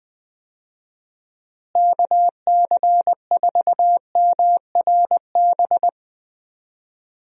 Morse code, heard as a keyed tone, about 20 wpm, 700 Hz. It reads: KC4MRB
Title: USA Call Signs - 6 Characters - Repeated 2x - 20wpm